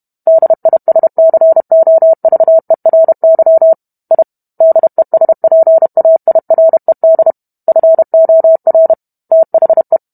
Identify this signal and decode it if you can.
Morse code, heard as a keyed tone, about 32 wpm, 650 Hz. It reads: DISCOVERY I DESPAIRED FOR THE